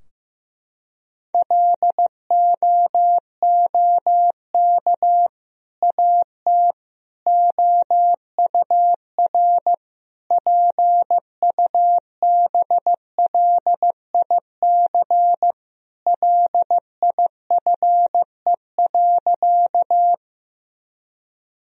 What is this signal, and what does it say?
Morse code, heard as a keyed tone, about 15 wpm, 700 Hz. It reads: LOOK AT OUR PUBLIC LIFE.
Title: Sentences from Top 1000 Words - Review Part 3 - 15wpm